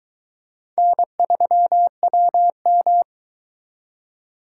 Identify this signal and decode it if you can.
Morse code, heard as a keyed tone, about 21 wpm, 700 Hz. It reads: N3WM